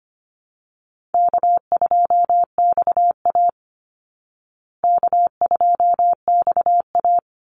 Morse code, heard as a keyed tone, about 25 wpm, 700 Hz. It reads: K2XA K2XA
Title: Easy Call Signs Repeated 2x - 25wpm